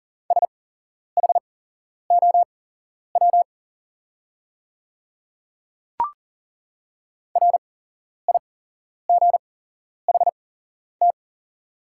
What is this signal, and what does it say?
Morse code, heard as a keyed tone, about 40 wpm, spaced as for 11 wpm, 690 Hz. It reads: SHOW RIGHT
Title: 8x Character Spacing - Top 200 Words - 40wpm